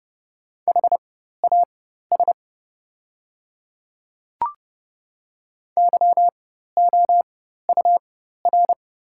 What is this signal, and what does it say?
Morse code, heard as a keyed tone, about 31 wpm, spaced as for 14 wpm, 690 Hz. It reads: HAS YOUR
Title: 4x Character Spacing - Top 100 Words - 30wpm